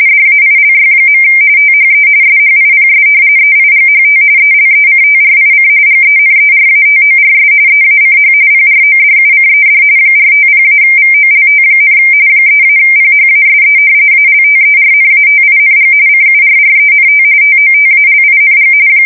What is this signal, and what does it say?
RTTY, about 45.45 baud. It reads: DAILY. THE SHIPS EIGHT ELECTRIC GENERATORS EACH PRODUCED 450-VOLT, 3-PHASE ALTERNATING CURRENT TOTALING
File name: fsk170 / battleshipnj